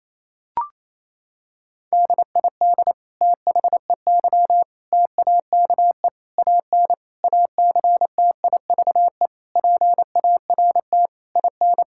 Morse code, heard as a keyed tone, about 28 wpm, 700 Hz. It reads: DID THEY TAKE AN ACTIVE PART IN
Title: Sentences from Top 1000 Words - Review Part 6 - 28wpm